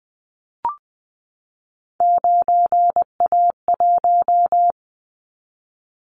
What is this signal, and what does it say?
Morse code, heard as a keyed tone, about 20 wpm, 700 Hz. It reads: 9A1
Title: Callsign Prefixes for Top 50 DXCC - 20wpm